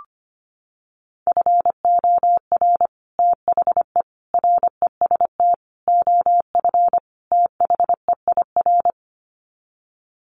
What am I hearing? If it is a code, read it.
Morse code, heard as a keyed tone, about 25 wpm, 700 Hz. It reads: FOR THE REST OF THEIR
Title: Sets of 5 Words Encore2 25wpm